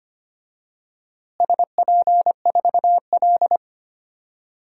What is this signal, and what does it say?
Morse code, heard as a keyed tone, about 25 wpm, 700 Hz. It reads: SP4L
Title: DXCC-100 Call Signs - 4 Characters - Repeated 3x - 25wpm